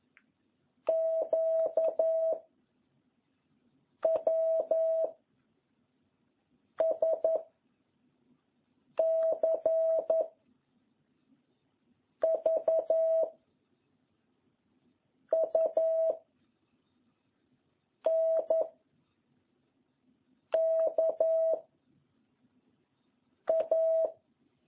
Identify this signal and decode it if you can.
Morse code, heard as a keyed tone, about 10 wpm, 650 Hz. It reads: Q W S C V U N K A